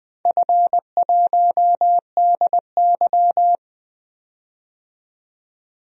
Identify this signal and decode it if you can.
Morse code, heard as a keyed tone, about 20 wpm, 700 Hz. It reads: F1DY
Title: DXCC-100 Call Signs - 4 Characters - 20wpm